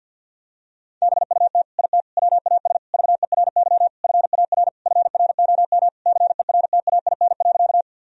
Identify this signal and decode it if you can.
Morse code, heard as a keyed tone, about 50 wpm, 700 Hz. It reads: BUT IT WAS VERY FAR FROM CERTAIN.
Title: Sentences from Top 1000 Words - Review Part 2 - 50wpm